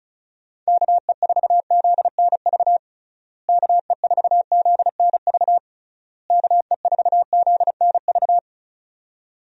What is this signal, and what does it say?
Morse code, heard as a keyed tone, about 36 wpm, 700 Hz. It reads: KE4ZNV KE4ZNV KE4ZNV
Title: USA Call Signs - 6 Characters - Repeated 3x - 35wpm